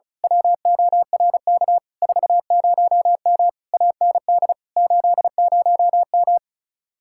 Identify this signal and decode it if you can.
Morse code, heard as a keyed tone, about 32 wpm, 700 Hz. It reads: WORK 40M AND 80M
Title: Speed-Racing Common QSO Phrases Encore - 35wpm